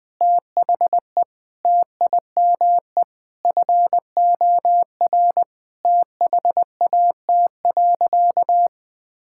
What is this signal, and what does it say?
Morse code, heard as a keyed tone, about 20 wpm, 700 Hz. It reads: THE TIME FOR THAT.
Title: Sentences with 5-Word Phrases - 20wpm